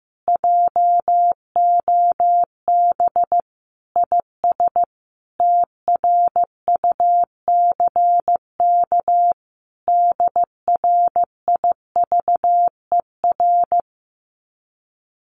Morse code, heard as a keyed tone, about 14 wpm, 700 Hz. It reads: JOB IS TRUCK DRIVER
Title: Common QSO Phrases Encore 15wpm